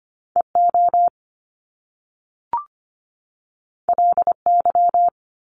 Morse code, heard as a keyed tone, about 25 wpm, 700 Hz. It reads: EO LY
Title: Top Two Letter Combinations in Words 25wpm